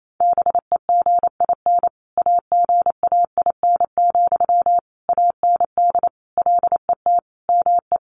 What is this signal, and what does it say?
Morse code, heard as a keyed tone, about 28 wpm, 700 Hz. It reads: BEGIN AGAIN, AND LET ME
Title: Sentences from Top 800 Words 28wpm